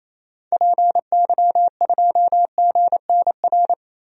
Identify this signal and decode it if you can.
Morse code, heard as a keyed tone, about 27 wpm, 700 Hz. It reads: PY2GNR
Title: DXCC-100 Call Signs - 6 Characters - Repeated 2x - 28wpm